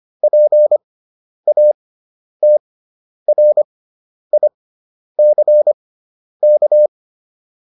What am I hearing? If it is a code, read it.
Morse code, heard as a keyed tone, about 25 wpm, spaced as for 10 wpm, 600 Hz. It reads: PATRICK